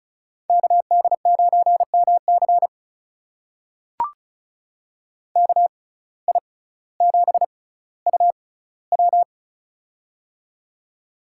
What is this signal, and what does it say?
Morse code, heard as a keyed tone, about 36 wpm, spaced as for 18 wpm, 700 Hz. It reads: KD9MC KI7UW
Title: USA Call Signs - 5 Characters - 6x Character Spacing - 35wpm